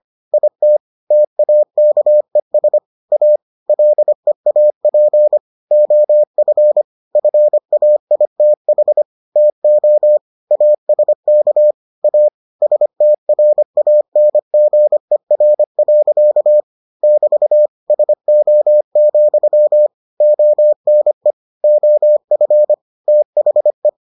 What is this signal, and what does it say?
Morse code, heard as a keyed tone, about 25 wpm, 600 Hz. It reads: IT TAKES A LEAP OF FAITH TO ASK A STRANGER. <BT> SO, ONE OF THE